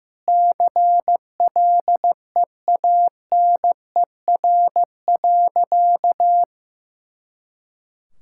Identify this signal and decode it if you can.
Morse code, heard as a keyed tone, about 16 wpm, 700 Hz. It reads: CLEANER.
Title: Pangram Challenge v2 - 15wpm